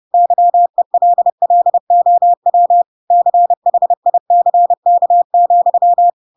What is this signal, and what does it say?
Morse code, heard as a keyed tone, about 30 wpm, 700 Hz. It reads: YELLOW CHICK,
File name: Pinocchio / Pinocchio ch5 30 30 ik1hge